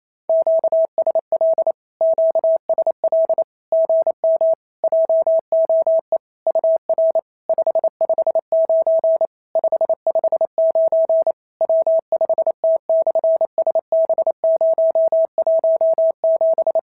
Morse code, heard as a keyed tone, about 28 wpm, 690 Hz. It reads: QSL QSL GM JOE UR 559 559 W5T/SB017
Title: SOTA QSO Practice 28wpm